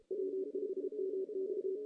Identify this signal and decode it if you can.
Morse code, heard as a keyed tone, about 32 wpm, 380 Hz. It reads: W5MX
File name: 2016 JAN CW NAQP / 2016 Jan naqp k5ot - 0-2